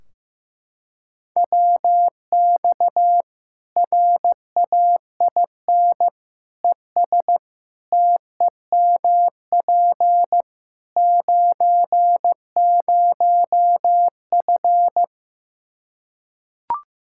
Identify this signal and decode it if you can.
Morse code, heard as a keyed tone, about 15 wpm, 700 Hz. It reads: WX RAIN ES TEMP 90F
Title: Practice QSO - Exchange 2 - Info - 15wpm